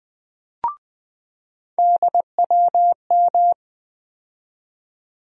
Morse code, heard as a keyed tone, about 20 wpm, 700 Hz. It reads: DWM
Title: USA Call Sign Postfixes - 3 Letters 20wpm